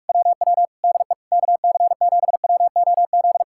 Morse code, heard as a keyed tone, about 60 wpm, spaced as for 52 wpm, 700 Hz. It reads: WW DE KC7WOZ
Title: Speed-Racing Common QSO Phrases - 45wpm